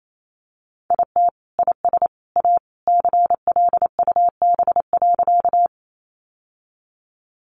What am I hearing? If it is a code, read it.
Morse code, heard as a keyed tone, about 28 wpm, 700 Hz. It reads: IT IS A CLUB.
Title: Sentences from Top 1200 Words 28wpm